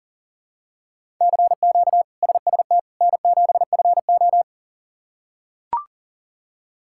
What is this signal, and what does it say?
Morse code, heard as a keyed tone, about 37 wpm, 700 Hz. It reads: CQ SST N7FO
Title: K1USN SST QSO Practice - 40wpm